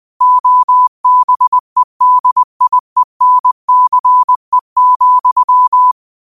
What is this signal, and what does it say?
Morse code, heard as a keyed tone, about 20 wpm, 1000 Hz. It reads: OBEDIENCE,
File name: qotdfiles / 20wpm